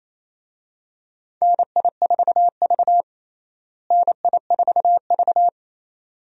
Morse code, heard as a keyed tone, about 28 wpm, 700 Hz. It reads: NI4V NI4V
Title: USA Call Signs - 4 Characters - Repeated 2x - 28wpm